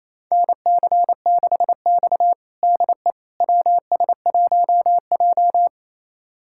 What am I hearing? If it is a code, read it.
Morse code, heard as a keyed tone, about 28 wpm, 700 Hz. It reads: NC6X DE WS1J
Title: USA Callsign DE Callsign - 4 Characters - 28wpm